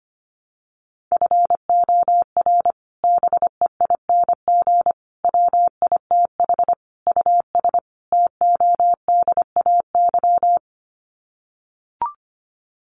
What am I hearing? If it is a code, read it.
Morse code, heard as a keyed tone, about 25 wpm, 700 Hz. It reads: FOR BEING WITH US TODAY
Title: Sets of 5 Words Encore2 25wpm